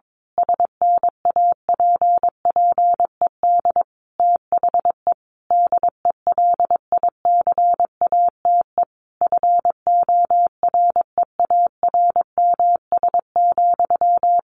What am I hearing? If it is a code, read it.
Morse code, heard as a keyed tone, about 22 wpm, 700 Hz. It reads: SNAPPED THE DELICATE FOREARMS,
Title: A Sound Of Thunder 22wpm